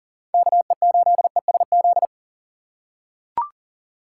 Morse code, heard as a keyed tone, about 39 wpm, 700 Hz. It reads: KE8ESZ
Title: USA Call Signs - 6 Characters - 3x Character Spacing - 40wpm